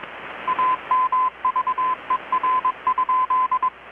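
Morse code, heard as a keyed tone, about 21 wpm, 1020 Hz. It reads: AMVER?
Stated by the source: Morse from KPH's Night of Nights XX maritime radiotelegraphy event, received in Chiapas, Mexico about 3,800 km away on a Kenwood R-600 with a grounded 10 m long-wire antenna.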